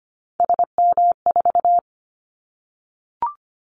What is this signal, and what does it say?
Morse code, heard as a keyed tone, about 25 wpm, 700 Hz. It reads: SM4
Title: Callsign Prefixes for Top 25 DXCC - 25wpm